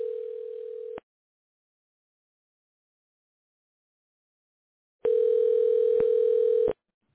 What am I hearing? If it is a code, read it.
Ringback tone (North America), 440 + 480 Hz.